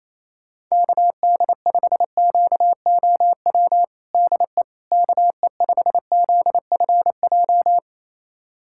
Morse code, heard as a keyed tone, about 28 wpm, 700 Hz. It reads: KD5QOW DE KE5ZFJ
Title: USA Callsign DE Callsign 28wpm